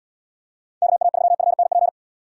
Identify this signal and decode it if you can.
Morse code, heard as a keyed tone, about 77 wpm, 700 Hz. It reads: KA3FNP